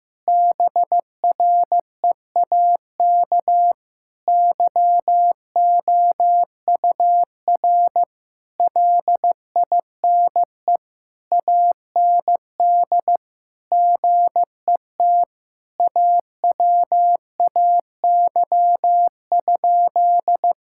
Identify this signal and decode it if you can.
Morse code, heard as a keyed tone, about 15 wpm, 700 Hz. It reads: BREAK YOUR LINE AND GET AWAY?
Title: Sentences from Top 1000 Words - Review Part 6 - 15wpm